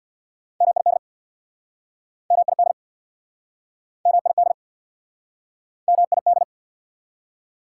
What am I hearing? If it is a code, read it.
Morse code, heard as a keyed tone, about 69 wpm, 690 Hz. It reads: MID MID MID MID